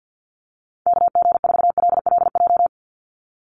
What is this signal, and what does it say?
Morse code, heard as a keyed tone, about 50 wpm, 700 Hz. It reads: KG4LLJ
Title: USA Call Signs 50wpm